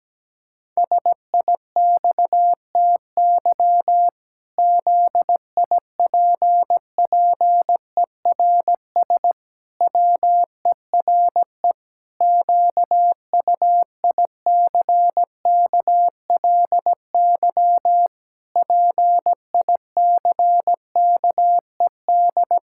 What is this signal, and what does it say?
Morse code, heard as a keyed tone, about 17 wpm, 700 Hz. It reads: SIXTY ZIPPERS WERE QUICKLY PICKED
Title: Pangram Challenge v2 - 17wpm